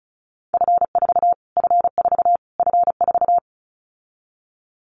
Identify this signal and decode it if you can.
Morse code, heard as a keyed tone, about 34 wpm, 700 Hz. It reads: F4 F4 F4
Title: Callsign Prefixes for Top 25 DXCC - Repeated 3x - 35wpm